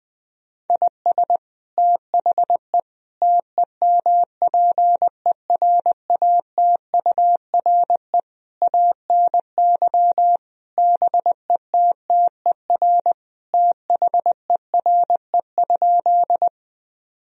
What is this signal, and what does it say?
Morse code, heard as a keyed tone, about 20 wpm, 700 Hz. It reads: IS THE TEMPERATURE ANY BETTER THERE?
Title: Sentences from Top 1000 Words - Review Part 4 - 20wpm